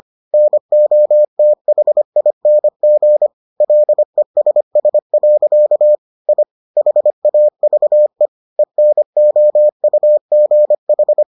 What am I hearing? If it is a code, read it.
Morse code, heard as a keyed tone, about 25 wpm, 600 Hz. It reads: NOTHING LESS. I HAVE ENOUGH